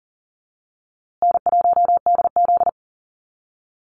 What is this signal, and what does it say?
Morse code, heard as a keyed tone, about 40 wpm, 700 Hz. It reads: N1DZ
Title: USA Call Signs Repeated 2x 40wpm